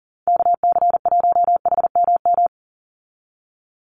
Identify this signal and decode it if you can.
Morse code, heard as a keyed tone, about 40 wpm, 700 Hz. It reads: KC1HMM
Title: USA Call Signs 40wpm